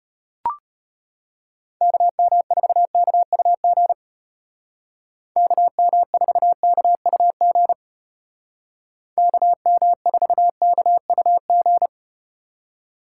Morse code, heard as a keyed tone, about 34 wpm, 700 Hz. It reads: KM4KUG KM4KUG KM4KUG